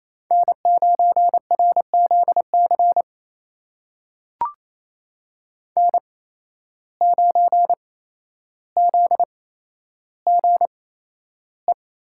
Morse code, heard as a keyed tone, about 28 wpm, spaced as for 12 wpm, 700 Hz. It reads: N9RZC N9ZGE
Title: USA Call Signs - 8x Character Spacing - 28wpm